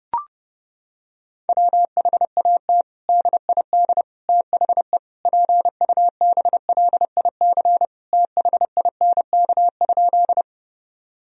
Morse code, heard as a keyed tone, about 31 wpm, 700 Hz. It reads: WHAT DID THE PUBLIC THINK?
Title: Sentences from Top 1000 Words - Review Part 3 - 30wpm